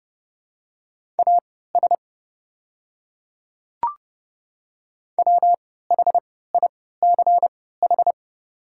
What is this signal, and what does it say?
Morse code, heard as a keyed tone, about 30 wpm, spaced as for 17 wpm, 700 Hz. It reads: AS WHICH
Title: 3x Character Spacing - Top 100 Words - 30wpm